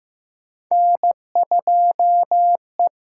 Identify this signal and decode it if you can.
Morse code, heard as a keyed tone, about 15 wpm, 700 Hz. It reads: N2E
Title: USA Call Signs - 8x Character Spacing - 15wpm